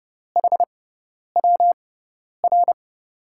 Morse code, heard as a keyed tone, about 32 wpm, spaced as for 10 wpm, 700 Hz. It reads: HWR
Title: USA Call Signs - 6 Characters - 6x Character Spacing - 30wpm